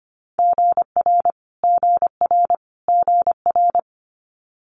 Morse code, heard as a keyed tone, about 25 wpm, 700 Hz. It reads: GR GR GR
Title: Multi-letter Phonemes Repeated 3x 25wpm